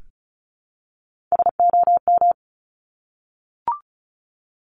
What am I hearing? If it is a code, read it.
Morse code, heard as a keyed tone, about 36 wpm, 700 Hz. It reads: SOM
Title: Top Three Letter Combinations in Words 35wpm